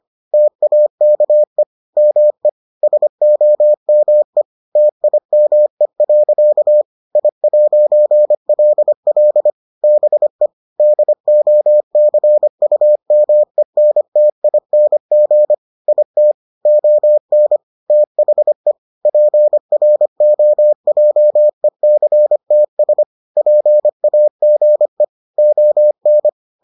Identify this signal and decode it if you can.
Morse code, heard as a keyed tone, about 25 wpm, 600 Hz. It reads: TAKE ME SOME TIME. I'LL BE DOCUMENTING IT ON THE PROJECTS PAGE ON